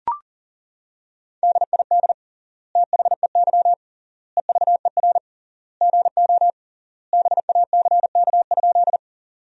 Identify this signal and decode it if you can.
Morse code, heard as a keyed tone, about 37 wpm, 700 Hz. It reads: DID THEY EVER GO BACK?
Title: 3x Word-Spacing - Sentences from Top 400 Words - 40wpm